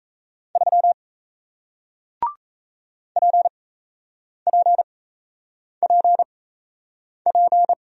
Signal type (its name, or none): Morse code